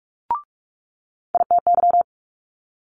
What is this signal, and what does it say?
Morse code, heard as a keyed tone, about 46 wpm, 700 Hz. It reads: ITY
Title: Multi-letter Phonemes Quick Fire 45wpm